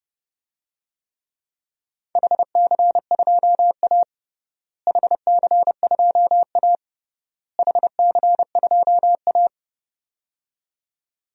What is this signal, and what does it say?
Morse code, heard as a keyed tone, about 31 wpm, 700 Hz. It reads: HC2A HC2A HC2A
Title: DXCC-100 Call Signs - 4 Characters - Repeated 3x - 30wpm